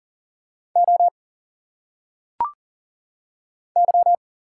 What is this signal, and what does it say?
Morse code, heard as a keyed tone, about 40 wpm, 700 Hz. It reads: O Y
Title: Single Letters - Phonetic Pronunciation 40wpm